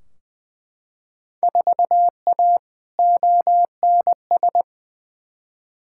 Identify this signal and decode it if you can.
Morse code, heard as a keyed tone, about 20 wpm, 700 Hz. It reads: 4A ONS
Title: Field Day Exchanges - 20wpm